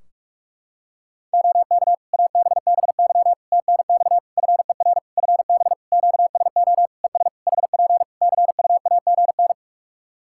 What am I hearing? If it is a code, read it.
Morse code, heard as a keyed tone, about 44 wpm, 700 Hz. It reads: OK ABBY TNX FER FB QSO ES HP CUAGN
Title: Practice QSO - Exchange 4 - Ending - 45wpm